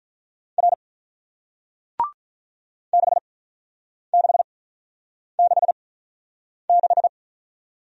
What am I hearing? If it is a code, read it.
Morse code, heard as a keyed tone, about 47 wpm, spaced as for 9 wpm, 700 Hz. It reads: R 6666